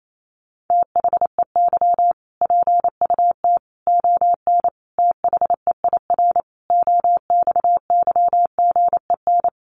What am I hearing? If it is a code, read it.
Morse code, heard as a keyed tone, about 28 wpm, 700 Hz. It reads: THEY PUT ON THEIR OXYGEN
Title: A Sound Of Thunder 28wpm